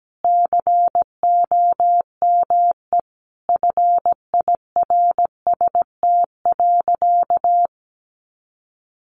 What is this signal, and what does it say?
Morse code, heard as a keyed tone, about 17 wpm, 700 Hz. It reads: COME FIRST.
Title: Sentences from Top 300 Words 17wpm